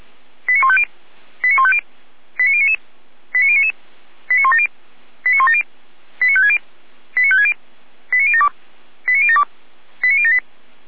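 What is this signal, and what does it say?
Five-tone selective call (ZVEI), heard as quick runs of five tones, tones of about 70 ms. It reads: 89150 89150 89099 89099 89170 89170 89560 89560 89072 89072 89078
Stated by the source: ZVEI five-tone selective call received on an SDR.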